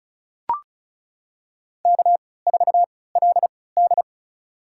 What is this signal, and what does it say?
Morse code, heard as a keyed tone, about 36 wpm, spaced as for 20 wpm, 700 Hz. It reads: K4LD
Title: USA Call Signs - 4 Characters - 3x Character Spacing - 35wpm